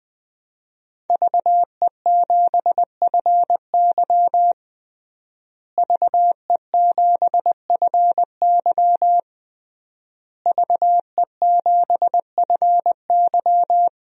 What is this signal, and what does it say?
Morse code, heard as a keyed tone, about 21 wpm, 700 Hz. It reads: VE7FY VE7FY VE7FY
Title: DXCC-100 Call Signs - 5 Characters - Repeated 3x - 20wpm